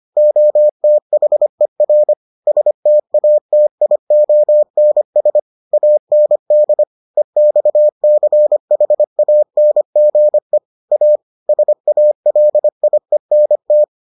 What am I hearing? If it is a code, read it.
Morse code, heard as a keyed tone, about 25 wpm, 600 Hz. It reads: OTHER STATIONS AND EXCHANGE A SALIENT